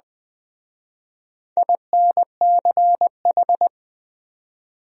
Morse code, heard as a keyed tone, about 23 wpm, 700 Hz. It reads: INCH